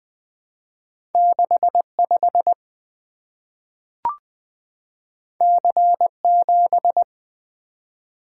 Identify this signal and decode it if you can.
Morse code, heard as a keyed tone, about 21 wpm, 700 Hz. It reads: 65 C7
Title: Two Letter-Number Combinations - 20wpm